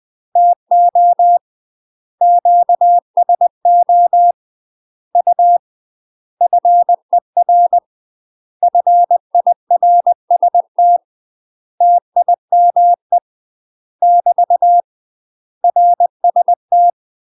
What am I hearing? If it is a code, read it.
Morse code, heard as a keyed tone, about 20 wpm, 700 Hz. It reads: TO QSO U FER FIRST TIME <BT> RST
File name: UK MASTER QSOs TRAINING / 20WPM / UK MASTER QSOs 0055